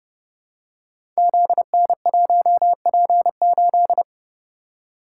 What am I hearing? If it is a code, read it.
Morse code, heard as a keyed tone, about 30 wpm, 700 Hz. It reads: ZN1P8
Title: Code Groups - 5 Letter:Numbers - 30wpm